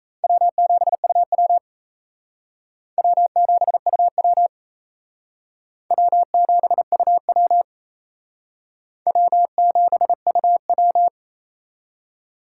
Morse code, heard as a keyed tone, about 35 wpm, 700 Hz. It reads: W7UW W7UW W7UW W7UW